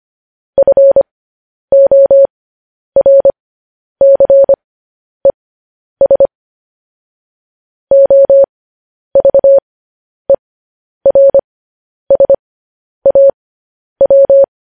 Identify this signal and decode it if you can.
Morse code, heard as a keyed tone, about 25 wpm, spaced as for 10 wpm, 550 Hz. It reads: FORCES OVERSAW
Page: news-headlines-in-morse-code-at-10-wpm